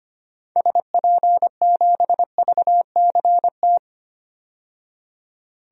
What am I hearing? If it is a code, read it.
Morse code, heard as a keyed tone, about 25 wpm, 700 Hz. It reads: SP7VCT
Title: DXCC-100 Call Signs - 6 Characters - 25wpm